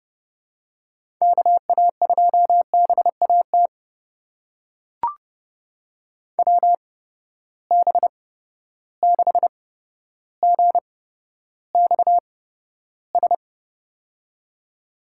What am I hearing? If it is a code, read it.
Morse code, heard as a keyed tone, about 32 wpm, spaced as for 13 wpm, 700 Hz. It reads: KA2BAT WB6GXS
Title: USA Call Signs - 6 Characters - 8x Character Spacing - 30wpm